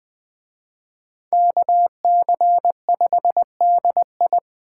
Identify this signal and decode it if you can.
Morse code, heard as a keyed tone, about 20 wpm, 700 Hz. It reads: KC5DI
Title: Speed Racing - USA 2x2 Call Signs 17wpm